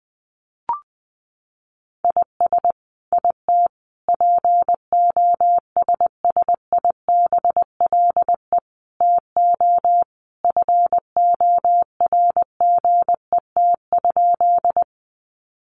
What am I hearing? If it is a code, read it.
Morse code, heard as a keyed tone, about 20 wpm, 700 Hz. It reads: IS IT POSSIBLE TO FORGET?
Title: Sentences from Top 1200 Words 20wpm